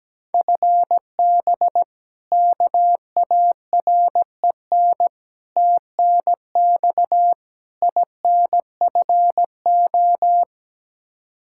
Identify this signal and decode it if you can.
Morse code, heard as a keyed tone, about 17 wpm, 700 Hz. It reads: FB KAREN TNX INFO
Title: K1USN SST QSO Practice 17wpm